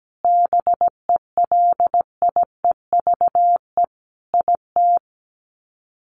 Morse code, heard as a keyed tone, about 17 wpm, 700 Hz. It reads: BELIEVE IT
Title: Sets of 5 Words 17wpm